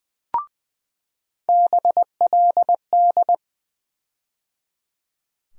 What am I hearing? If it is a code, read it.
Morse code, heard as a keyed tone, about 20 wpm, 700 Hz. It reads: BLD
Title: USA Call Sign Postfixes - 3 Letters 20wpm